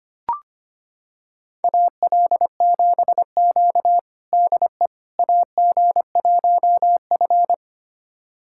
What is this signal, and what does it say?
Morse code, heard as a keyed tone, about 25 wpm, 700 Hz. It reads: AL7Q DE AG1F
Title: USA Callsign DE Callsign - 4 Characters - 25wpm